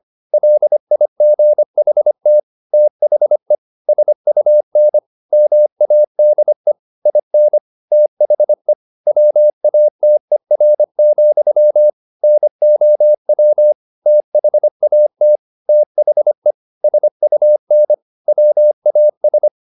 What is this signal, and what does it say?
Morse code, heard as a keyed tone, about 25 wpm, 600 Hz. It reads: LIGHT THE SUN MADE IN THE WATER, NOW THAT THE SUN WAS
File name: the old man and the sea part 1 25wpm